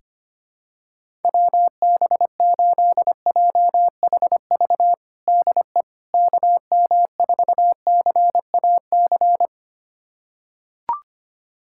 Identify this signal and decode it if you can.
Morse code, heard as a keyed tone, about 25 wpm, 700 Hz. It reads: WB8JHV DE KM4CAC
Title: USA Callsign DE Callsign 25wpm